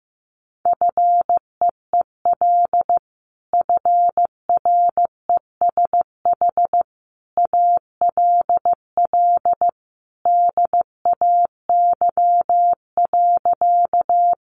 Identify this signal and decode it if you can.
Morse code, heard as a keyed tone, about 15 wpm, 700 Hz. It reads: FEEL FRESH ALL DAY.
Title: Sentences from Top 1200 Words 15wpm